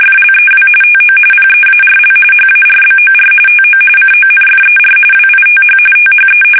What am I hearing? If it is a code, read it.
RTTY, about 45.45 baud. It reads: FLEW TO PEARL HARBOR TO CONFER WITH THE
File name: fsk850 / battleshipnj